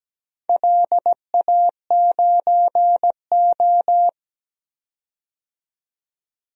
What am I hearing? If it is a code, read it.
Morse code, heard as a keyed tone, about 17 wpm, 700 Hz. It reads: LA9O
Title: DXCC-100 Call Signs - 4 Characters - 17wpm